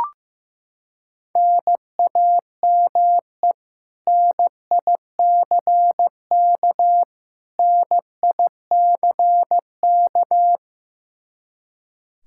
Morse code, heard as a keyed tone, about 15 wpm, 700 Hz. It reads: NAME NICK NICK
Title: Practice QSO - Exchange 1 - Beginning - 15wpm